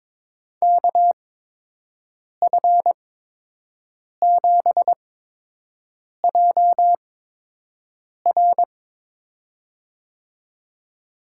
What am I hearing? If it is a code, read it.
Morse code, heard as a keyed tone, about 22 wpm, spaced as for 6 wpm, 700 Hz. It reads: KF7JR
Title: USA Call Signs - 8x Character Spacing - 22wpm